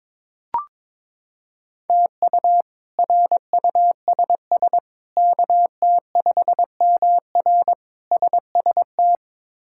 Morse code, heard as a keyed tone, about 22 wpm, 700 Hz. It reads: TU RUSS KT5MR SST
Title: K1USN SST QSO Practice - 22wpm